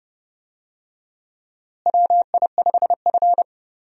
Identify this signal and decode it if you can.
Morse code, heard as a keyed tone, about 32 wpm, 700 Hz. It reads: WI5F
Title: USA Call Signs - 4 Characters - Repeated 3x - 30wpm